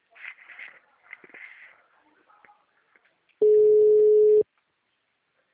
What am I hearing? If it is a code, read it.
Ringback tone (Europe, 425 Hz).